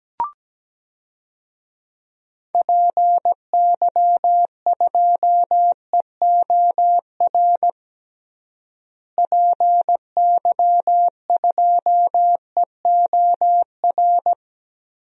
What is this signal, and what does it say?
Morse code, heard as a keyed tone, about 17 wpm, 700 Hz. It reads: PY2EOR PY2EOR
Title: DXCC-100 Call Signs - 6 Characters - Repeated 2x - 17wpm